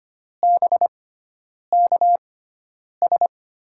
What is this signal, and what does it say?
Morse code, heard as a keyed tone, about 25 wpm, spaced as for 9 wpm, 700 Hz. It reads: BKS
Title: USA Call Signs - 6 Characters - 6x Character Spacing - 25wpm